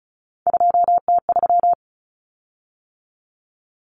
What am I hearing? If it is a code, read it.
Morse code, heard as a keyed tone, about 34 wpm, 700 Hz. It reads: 2T3
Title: Common Cut Numbers - 3-Digit 35wpm